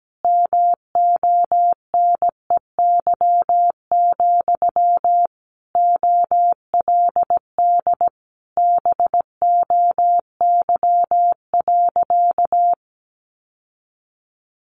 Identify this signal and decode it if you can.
Morse code, heard as a keyed tone, about 17 wpm, 700 Hz. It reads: MONEY, OLD BOY.
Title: Sentences from Top 1500 Words 17wpm